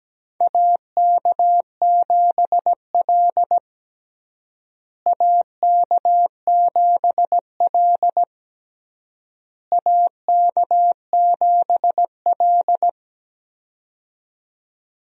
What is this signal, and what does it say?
Morse code, heard as a keyed tone, about 17 wpm, 700 Hz. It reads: AK7L AK7L AK7L
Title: USA Call Signs - 4 Characters - Repeated 3x - 17wpm